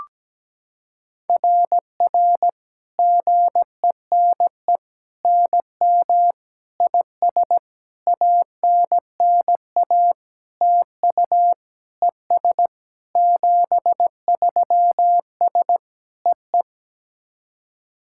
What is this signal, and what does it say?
Morse code, heard as a keyed tone, about 17 wpm, 700 Hz. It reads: RR GENE NM IS ANNA TU ES 73S EE
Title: SOTA QSO Practice 17wpm